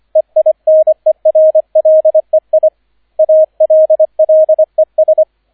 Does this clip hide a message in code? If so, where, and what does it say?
Yes: Morse code from 0.2 to 5.2 s, heard as a keyed tone, about 25 wpm, 630 Hz, with radio noise. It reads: EINERLEI ALLES